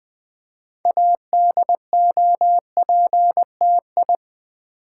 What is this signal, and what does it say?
Morse code, heard as a keyed tone, about 20 wpm, 700 Hz. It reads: ADOPTI
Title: Build A Word - 3 Letters At A Time - 8-Letter Words - 20wpm